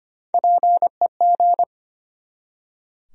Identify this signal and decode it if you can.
Morse code, heard as a keyed tone, about 25 wpm, 700 Hz. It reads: PEG
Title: USA Common Names - Max 5 Letters - 25wpm